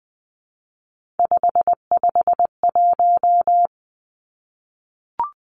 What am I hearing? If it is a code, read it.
Morse code, heard as a keyed tone, about 20 wpm, 700 Hz. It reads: 551
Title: Three Challenging Letter-Number Combinations 20wpm